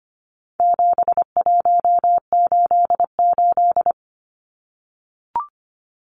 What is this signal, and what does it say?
Morse code, heard as a keyed tone, about 25 wpm, 700 Hz. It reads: 7188
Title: Numbers - 4-Digit 25wpm